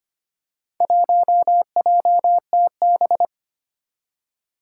Morse code, heard as a keyed tone, about 25 wpm, 700 Hz. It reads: 1JTB
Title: Four Challenging Letter-Number Combinations 25wpm